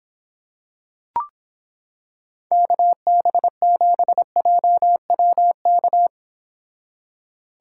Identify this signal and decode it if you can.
Morse code, heard as a keyed tone, about 26 wpm, 700 Hz. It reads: KB7JWK